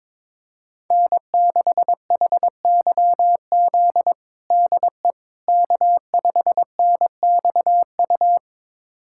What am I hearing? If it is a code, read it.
Morse code, heard as a keyed tone, about 22 wpm, 700 Hz. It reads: N6HYZ DE K5NXU
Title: USA Callsign DE Callsign - 5 Characters - 22wpm